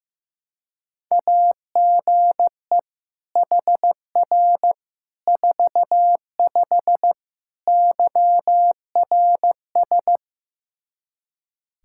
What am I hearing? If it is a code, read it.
Morse code, heard as a keyed tone, about 15 wpm, 700 Hz. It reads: AGE HR 45 YRS
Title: K1USN SST QSO Practice 15wpm